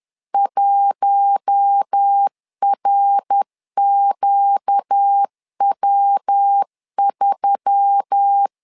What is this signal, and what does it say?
Morse code, heard as a keyed tone, about 10 wpm, 800 Hz. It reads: 1RQW3
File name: LEZIONI / 4 Q1S3 10wpm 2427